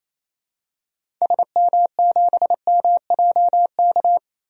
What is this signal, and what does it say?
Morse code, heard as a keyed tone, about 28 wpm, 700 Hz. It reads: SM7MJK
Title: DXCC-100 Call Signs - 6 Characters - 28wpm